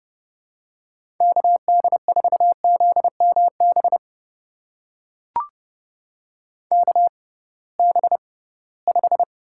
Morse code, heard as a keyed tone, about 31 wpm, spaced as for 20 wpm, 700 Hz. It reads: KD4ZMB KB5
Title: USA Call Signs - 6 Characters - 6x Character Spacing - 30wpm